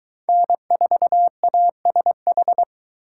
Morse code, heard as a keyed tone, about 21 wpm, 700 Hz. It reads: N4ASH